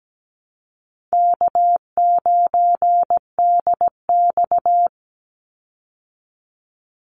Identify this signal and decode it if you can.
Morse code, heard as a keyed tone, about 17 wpm, 700 Hz. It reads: K9DX
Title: Easy Call Signs Repeated 2x - 17wpm